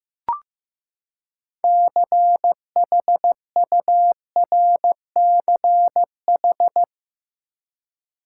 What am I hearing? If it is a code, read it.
Morse code, heard as a keyed tone, about 16 wpm, 700 Hz. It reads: CHURCH
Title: Six Letter Words - 15wpm